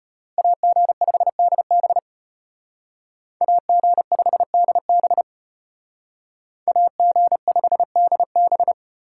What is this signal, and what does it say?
Morse code, heard as a keyed tone, about 33 wpm, 700 Hz. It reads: AG5DB AG5DB AG5DB